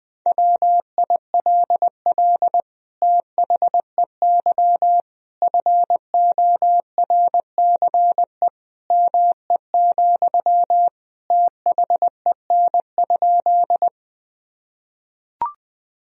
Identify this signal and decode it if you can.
Morse code, heard as a keyed tone, about 20 wpm, 700 Hz. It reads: WILL THEY FORCE ME, THEN?
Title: Sentences from Top 1000 Words - Review Part 3 - 20wpm